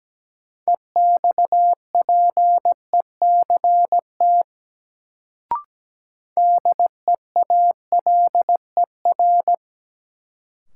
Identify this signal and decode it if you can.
Morse code, heard as a keyed tone, about 17 wpm, 700 Hz. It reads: EXPECT DEALER
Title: Sentences from Top 1100 Words - New Word Drill Alt - 17wpm